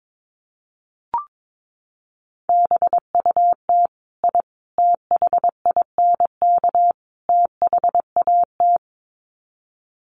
Morse code, heard as a keyed tone, about 22 wpm, 700 Hz. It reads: BUT I THINK THAT
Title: Sets of 4 Words 22wpm